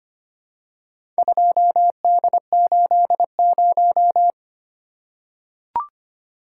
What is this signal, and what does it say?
Morse code, heard as a keyed tone, about 25 wpm, 690 Hz. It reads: 2D80
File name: practice / 25wpm / Four Letter-Number Combinations - 25wpm